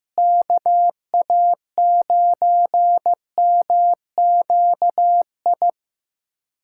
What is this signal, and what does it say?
Morse code, heard as a keyed tone, about 14 wpm, 700 Hz. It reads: KA9MQI
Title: USA Callsign DE Callsign - 6 Characters - 15wpm